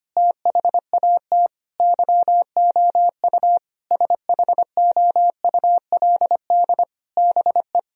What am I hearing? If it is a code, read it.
Morse code, heard as a keyed tone, about 25 wpm, 700 Hz. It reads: THAT YOU SHOULD BE
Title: Sentences from Top 1000 Words - Review Part 1 - 25wpm